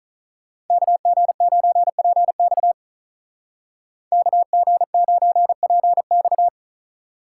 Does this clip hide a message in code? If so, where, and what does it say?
Yes: Morse code from 0.7 to 6.5 s, heard as a keyed tone, about 45 wpm, 700 Hz. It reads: KG9PX KG9PX